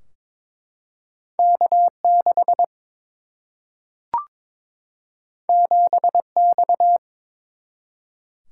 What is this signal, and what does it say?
Morse code, heard as a keyed tone, about 22 wpm, 700 Hz. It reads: K6 7X
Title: Single-Letter Single-Number 22wpm